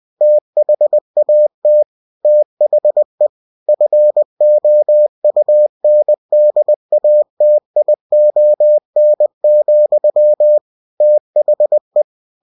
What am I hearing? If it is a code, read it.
Morse code, heard as a keyed tone, about 20 wpm, 600 Hz. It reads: THAT THE FOUNDATION, THE